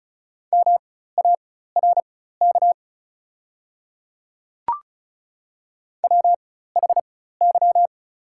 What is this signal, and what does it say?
Morse code, heard as a keyed tone, about 36 wpm, spaced as for 16 wpm, 700 Hz. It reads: MARK WHY
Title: 4x Character Spacing - Top 300 Words - 35wpm